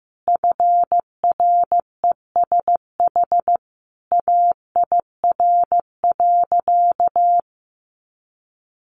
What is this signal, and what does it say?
Morse code, heard as a keyed tone, about 15 wpm, 700 Hz. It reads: FRESH AIR.
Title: Sentences from Top 1200 Words 15wpm